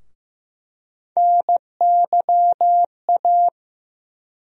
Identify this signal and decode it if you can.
Morse code, heard as a keyed tone, about 15 wpm, 700 Hz. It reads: NYA
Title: USA Call Sign Postfixes - 3 Letters 15wpm